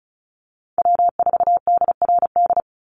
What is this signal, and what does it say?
Morse code, heard as a keyed tone, about 34 wpm, 700 Hz. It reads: W4DRD
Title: USA Call Signs 35wpm